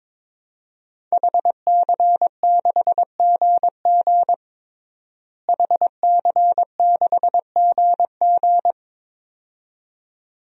Morse code, heard as a keyed tone, about 22 wpm, 700 Hz. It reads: HC6GG HC6GG
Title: DXCC-100 Call Signs - 5 Characters - Repeated 2x - 22wpm